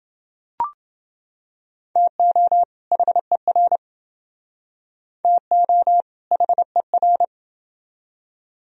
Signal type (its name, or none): Morse code